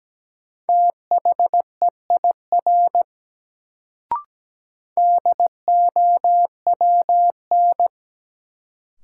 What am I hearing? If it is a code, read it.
Morse code, heard as a keyed tone, about 17 wpm, 700 Hz. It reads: THEIR DOWN
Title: Sentences from Top 300 Words - New Word Drill - 17wpm